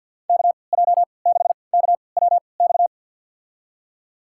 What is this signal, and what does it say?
Morse code, heard as a keyed tone, about 52 wpm, spaced as for 29 wpm, 700 Hz. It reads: KJ6KWX
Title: USA Call Signs - 6 Characters - 3x Character Spacing - 50wpm